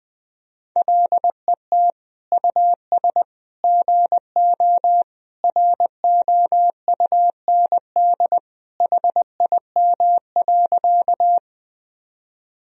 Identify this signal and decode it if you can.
Morse code, heard as a keyed tone, about 20 wpm, 700 Hz. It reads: LET US GO ROUND HIM.
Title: Sentences from Top 1000 Words - Review Part 1 - 20wpm